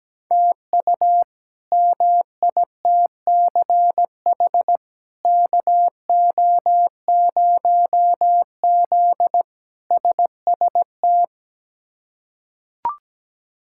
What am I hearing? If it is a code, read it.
Morse code, heard as a keyed tone, about 17 wpm, 700 Hz. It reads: TU MITCH KO0Z SST
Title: K1USN SST QSO Practice - 17wpm